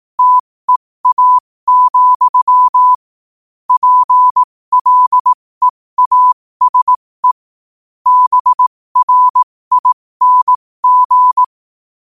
Morse code, heard as a keyed tone, about 18 wpm, spaced as for 15 wpm, 1000 Hz. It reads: TEA, PLEASE BRING
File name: qotdfiles / 15wpm